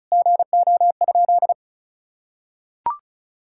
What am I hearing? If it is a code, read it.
Morse code, heard as a keyed tone, about 36 wpm, 700 Hz. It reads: GO?
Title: Sentences from Top 1000 Words - Review Part 5 - 35wpm